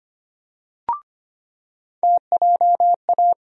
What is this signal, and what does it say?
Morse code, heard as a keyed tone, about 25 wpm, 700 Hz. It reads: TJA
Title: USA Call Sign Postfixes - 3 Letters 25wpm